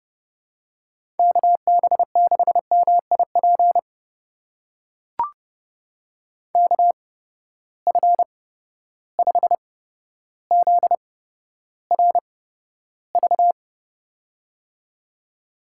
Morse code, heard as a keyed tone, about 30 wpm, spaced as for 13 wpm, 700 Hz. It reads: KB6MIP KF5ZRV
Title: USA Call Signs - 8x Character Spacing - 30wpm